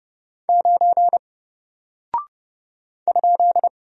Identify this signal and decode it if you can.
Morse code, heard as a keyed tone, about 30 wpm, 700 Hz. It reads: 9 ?
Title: Single Character (Letter:Number:Punctuation:Prosign) Alt 30wpm